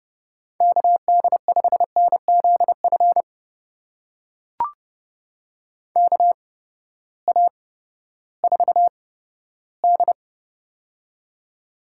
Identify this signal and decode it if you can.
Morse code, heard as a keyed tone, about 31 wpm, spaced as for 15 wpm, 700 Hz. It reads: KD5NZF KA4D
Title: USA Call Signs - 8x Character Spacing - 30wpm